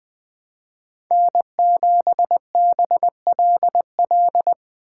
Morse code, heard as a keyed tone, about 20 wpm, 700 Hz. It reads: N7BLL
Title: USA Call Signs - 5 Characters - 6x Character Spacing - 20wpm